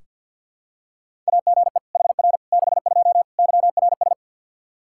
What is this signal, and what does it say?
Morse code, heard as a keyed tone, about 46 wpm, 700 Hz. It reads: AGE HR 62 YRS
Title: Practice QSO - Exchange 3 - More Info - 50wpm